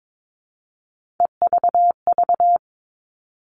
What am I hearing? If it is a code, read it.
Morse code, heard as a keyed tone, about 22 wpm, 700 Hz. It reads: EVV
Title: All Cut Numbers - 3-Digit 22wpm